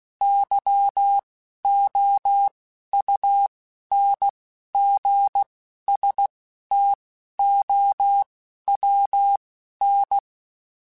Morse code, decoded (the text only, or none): YOUNGSTOWN